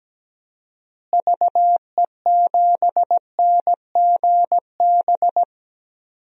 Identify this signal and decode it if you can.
Morse code, heard as a keyed tone, about 17 wpm, 700 Hz. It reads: VE7NGB
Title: DXCC-100 Call Signs - 6 Characters - Repeated 2x - 17wpm